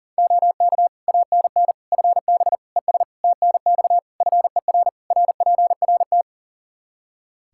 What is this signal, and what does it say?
Morse code, heard as a keyed tone, about 37 wpm, 700 Hz. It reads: OK ANN FB ES TNX FER RPRT
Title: K1USN SST QSO Practice 40wpm